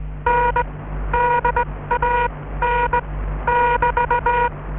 Morse code, heard as a keyed tone, about 19 wpm, spaced as for 15 wpm, 960 Hz, with noise. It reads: N DAN <BT>